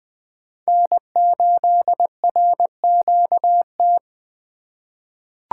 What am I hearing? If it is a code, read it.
Morse code, heard as a keyed tone, about 20 wpm, 700 Hz. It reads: N8RQT
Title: USA Call Signs - 5 Characters - 4x Character Spacing - 20wpm